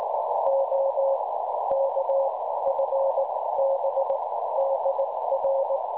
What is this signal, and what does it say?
Morse code, heard as a keyed tone, about 18 wpm, spaced as for 13 wpm, 550 Hz, with noise. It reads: OK FB DR
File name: cwronde / websdr recording .6kHz